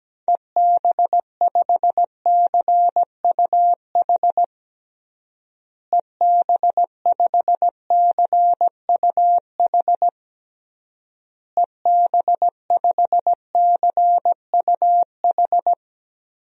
Morse code, heard as a keyed tone, about 17 wpm, 700 Hz. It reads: EB5CUH EB5CUH EB5CUH
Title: DXCC-100 Call Signs - 6 Characters - Repeated 3x - 17wpm